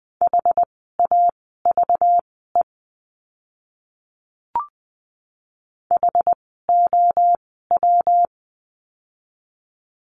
Morse code, heard as a keyed tone, about 20 wpm, spaced as for 15 wpm, 700 Hz. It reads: HAVE HOW
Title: 230 Lesson 37f - Letter x - Words 20wpm